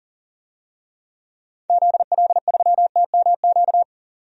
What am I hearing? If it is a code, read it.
Morse code, heard as a keyed tone, about 40 wpm, 700 Hz. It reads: ZL3TMQ
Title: DXCC-100 Call Signs - 6 Characters - 40wpm